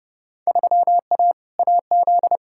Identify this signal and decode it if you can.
Morse code, heard as a keyed tone, about 30 wpm, 700 Hz. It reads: 3A AZ
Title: Field Day Exchanges - 30wpm